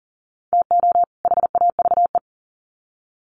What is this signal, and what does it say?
Morse code, heard as a keyed tone, about 39 wpm, 700 Hz. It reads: TO HAVE
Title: Sets of 2 Words 40wpm